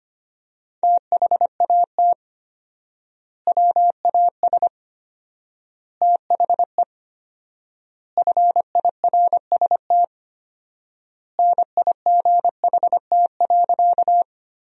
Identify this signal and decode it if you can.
Morse code, heard as a keyed tone, about 25 wpm, 700 Hz. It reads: THAT WAS THE FIRST NIGHT.
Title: 4x Word-Spacing - Sentences from Top 300 Words - 25wpm